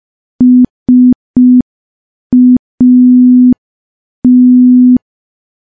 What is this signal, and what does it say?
Morse code, heard as a keyed tone, about 5 wpm, 250 Hz. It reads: SAT
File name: morse / morse-training-05WPM